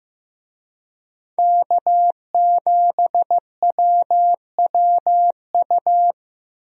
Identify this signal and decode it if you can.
Morse code, heard as a keyed tone, about 15 wpm, 700 Hz. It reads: K7WWU
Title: USA Call Signs - 5 Characters - Repeated 2x - 15wpm